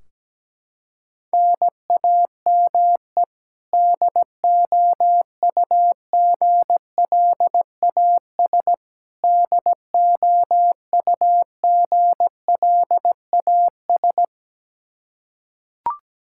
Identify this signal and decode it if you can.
Morse code, heard as a keyed tone, about 17 wpm, 700 Hz. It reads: NAME DOUGLAS DOUGLAS
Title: Practice QSO - Exchange 1 - Beginning - 17wpm